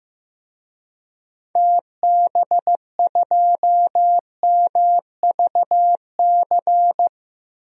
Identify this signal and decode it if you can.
Morse code, heard as a keyed tone, about 15 wpm, 700 Hz. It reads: TB2MVC
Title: DXCC-100 Call Signs - 6 Characters - 15wpm